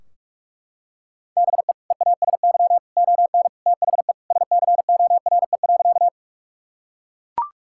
Morse code, heard as a keyed tone, about 41 wpm, 700 Hz. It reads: BE EASY ON THE SCORE.
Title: Sentences from Top 1600 Words 45wpm